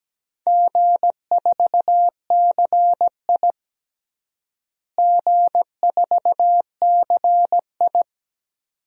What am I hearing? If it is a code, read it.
Morse code, heard as a keyed tone, about 17 wpm, 700 Hz. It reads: G4CI G4CI
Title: DXCC-100 Call Signs - 4 Characters - Repeated 2x - 17wpm